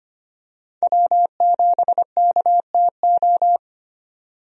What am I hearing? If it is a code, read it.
Morse code, heard as a keyed tone, about 25 wpm, 700 Hz. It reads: W7KTO
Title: USA Call Signs - 5 Characters - 25wpm